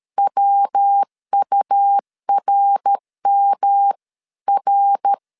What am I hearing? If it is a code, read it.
Morse code, heard as a keyed tone, about 13 wpm, 800 Hz. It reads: WURMR
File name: CW / DET / 1 3 1500C S B 12wpm 2828